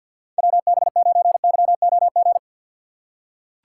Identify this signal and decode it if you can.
Morse code, heard as a keyed tone, about 47 wpm, 700 Hz. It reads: WB9YOG
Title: USA Call Signs - 6 Characters - 3x Character Spacing - 50wpm